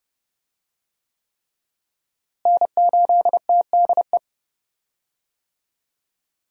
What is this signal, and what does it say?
Morse code, heard as a keyed tone, about 30 wpm, 700 Hz. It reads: N8TDE
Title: USA Call Signs - 5 Characters - Repeated 3x - 30wpm